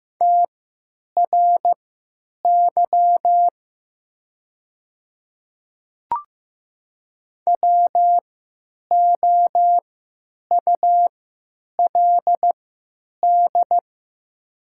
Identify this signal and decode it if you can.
Morse code, heard as a keyed tone, about 15 wpm, spaced as for 9 wpm, 700 Hz. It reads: TRY WOULD
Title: 3x Character Spacing - Top 200 Words - 15wpm